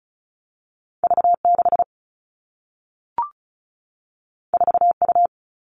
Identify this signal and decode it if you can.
Morse code, heard as a keyed tone, about 36 wpm, 700 Hz. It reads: V6 4U
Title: All Cut Numbers - 2-Digit 35wpm